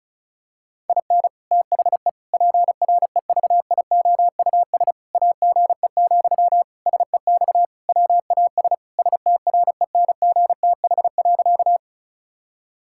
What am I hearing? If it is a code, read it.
Morse code, heard as a keyed tone, about 36 wpm, 700 Hz. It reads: IN THE PREVIOUS AGE, SEX WAS STRENGTH.
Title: Sentences from Top 1600 Words 35wpm